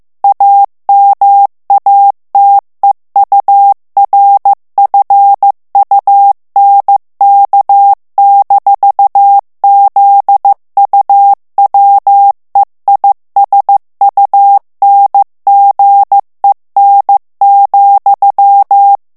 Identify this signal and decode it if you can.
Morse code, heard as a keyed tone, about 15 wpm, 800 Hz. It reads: AMATEURFUNK-ZUWEISUNGEN,
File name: morse mp3 / ALLEMAND texte clair 1 vitesse 14 mots minute